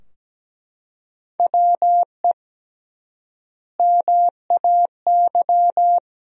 Morse code, heard as a keyed tone, about 17 wpm, 700 Hz. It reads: WE MAY
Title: 3x Word-Spacing - Sentences from Top 300 Words - 17wpm